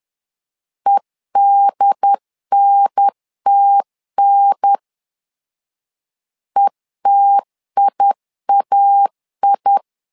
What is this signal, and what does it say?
Morse code, heard as a keyed tone, about 10 wpm, 800 Hz. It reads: EDNTN ETIAI
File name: LEZIONI / 2 UHDN / 2 UHDN 10wpm 1904